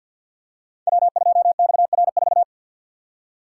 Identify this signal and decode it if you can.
Morse code, heard as a keyed tone, about 52 wpm, 700 Hz. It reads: W2XR4
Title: Code Groups - 5 Letter:Numbers - 50wpm